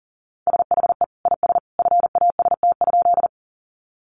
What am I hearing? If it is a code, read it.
Morse code, heard as a keyed tone, about 40 wpm, 700 Hz. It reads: SHE IS FAST?
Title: Sentences from Top 400 Words 40wpm